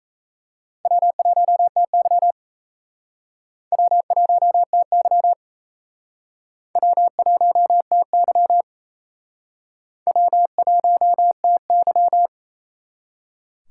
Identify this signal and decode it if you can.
Morse code, heard as a keyed tone, about 35 wpm, 700 Hz. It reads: W1TY W1TY W1TY W1TY
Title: Speed Racing - USA 1x2 Call Signs 28wpm